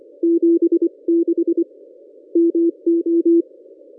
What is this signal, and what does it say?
Morse code, heard as a keyed tone, about 25 wpm, 340 Hz. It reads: 76 MO
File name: contests / ss-cw-k5tr-op-n5rz -1200